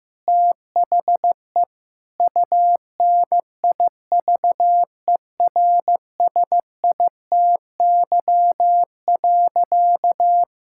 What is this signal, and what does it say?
Morse code, heard as a keyed tone, about 15 wpm, 700 Hz. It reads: THE UNIVERSITY.
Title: Sentences from Top 1000 Words - Review Part 5 - 15wpm